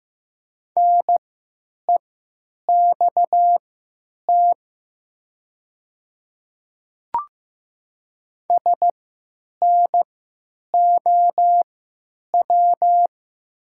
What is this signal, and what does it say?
Morse code, heard as a keyed tone, about 15 wpm, spaced as for 9 wpm, 700 Hz. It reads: NEXT SNOW
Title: 3x Character Spacing - Top 500 Words - 15wpm